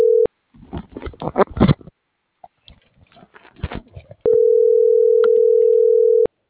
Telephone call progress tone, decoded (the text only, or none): ringback tone